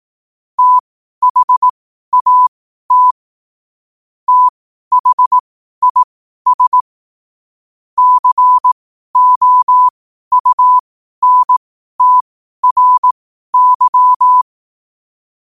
Morse code, heard as a keyed tone, about 18 wpm, spaced as for 12 wpm, 1000 Hz. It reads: THAT THIS COUNTRY
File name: qotdfiles / 12wpm